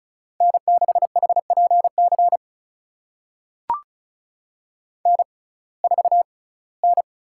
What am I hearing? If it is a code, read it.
Morse code, heard as a keyed tone, about 34 wpm, spaced as for 21 wpm, 700 Hz. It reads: N6HPC N4N
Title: USA Call Signs - 5 Characters - 6x Character Spacing - 35wpm